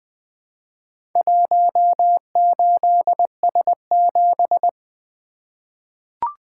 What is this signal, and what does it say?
Morse code, heard as a keyed tone, about 20 wpm, 700 Hz. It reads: 18S7
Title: Four Letter-Number Combinations - 20wpm